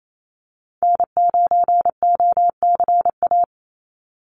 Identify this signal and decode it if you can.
Morse code, heard as a keyed tone, about 28 wpm, 700 Hz. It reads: N9OCA
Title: USA Call Signs 28wpm